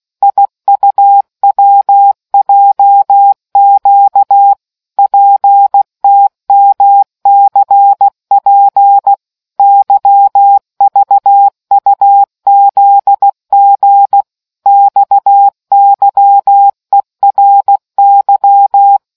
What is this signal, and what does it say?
Morse code, heard as a keyed tone, about 16 wpm, 800 Hz. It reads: IUWJQ PTMCP YVUZG XYERY